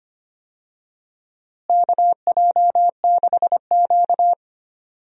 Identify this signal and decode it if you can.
Morse code, heard as a keyed tone, about 25 wpm, 700 Hz. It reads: KJ6Q
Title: USA Call Signs - 4 Characters - Repeated 3x - 25wpm